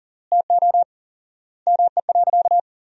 Morse code, heard as a keyed tone, about 40 wpm, 700 Hz. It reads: TO ME.
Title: 4x Word-Spacing - Sentences from Top 500 Words - 40wpm